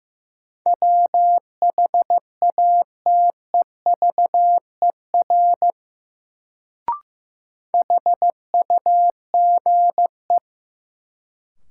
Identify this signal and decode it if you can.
Morse code, heard as a keyed tone, about 15 wpm, 700 Hz. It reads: WHATEVER HUGE
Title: Sentences from Top 1100 Words - New Word Drill - 15wpm